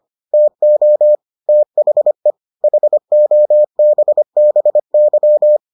Morse code, heard as a keyed tone, about 25 wpm, 600 Hz. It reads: TO THE HOBBY